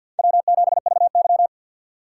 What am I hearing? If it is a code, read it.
Morse code, heard as a keyed tone, about 52 wpm, 700 Hz. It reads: W7VY
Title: USA Call Signs - 3x Character Spacing - 50wpm